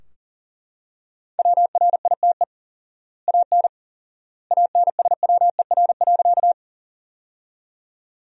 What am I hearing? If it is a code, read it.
Morse code, heard as a keyed tone, about 37 wpm, 700 Hz. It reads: WRITE AN ANSWER.
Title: 4x Word-Spacing - Sentences from Top 300 Words - 40wpm